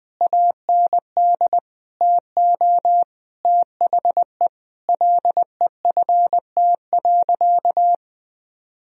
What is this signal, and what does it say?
Morse code, heard as a keyed tone, about 19 wpm, 700 Hz. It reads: AND TO THE LEFT.
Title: Sentences from Top 1000 Words - Review Part 1 - 20wpm